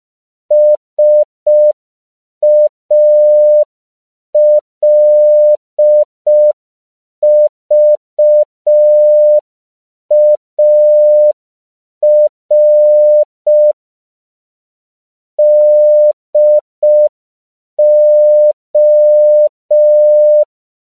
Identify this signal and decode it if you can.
Morse code, heard as a keyed tone, about 5 wpm, 600 Hz. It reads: SALVAR DO